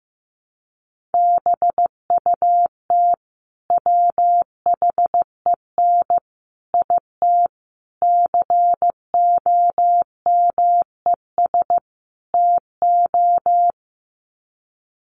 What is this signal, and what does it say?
Morse code, heard as a keyed tone, about 15 wpm, 700 Hz. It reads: BUT WHEN IT COMES TO
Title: Sets of 5 Words 15wpm